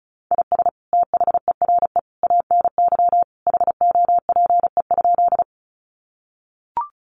Morse code, heard as a keyed tone, about 34 wpm, 700 Hz. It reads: IS THERE ANY HOPE?
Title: Sentences from Top 1000 Words 35wpm